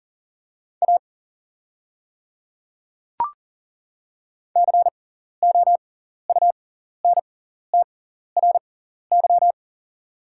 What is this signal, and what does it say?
Morse code, heard as a keyed tone, about 39 wpm, spaced as for 14 wpm, 700 Hz. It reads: A COUNTRY
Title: 6x Character Spacing - Top 300 Words - 40wpm